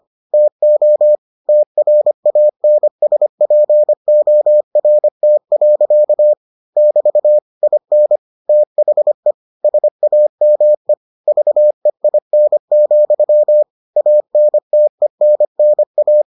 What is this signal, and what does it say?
Morse code, heard as a keyed tone, about 25 wpm, 600 Hz. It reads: TO TRANSPORT. <BT> IN THE SAME VEIN, ANTENNA